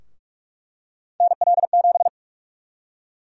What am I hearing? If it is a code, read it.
Morse code, heard as a keyed tone, about 46 wpm, 700 Hz. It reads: NL7
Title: USA Call Sign Prefixes - 45wpm